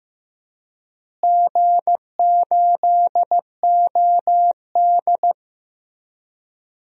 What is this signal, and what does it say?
Morse code, heard as a keyed tone, about 15 wpm, 700 Hz. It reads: G8OD
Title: DXCC-100 Call Signs - 4 Characters - Repeated 3x - 15wpm